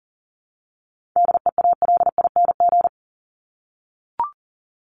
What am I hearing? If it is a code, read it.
Morse code, heard as a keyed tone, about 40 wpm, 700 Hz. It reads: DEALING
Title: ING Suffix - Top 100 Words with Base Words 40wpm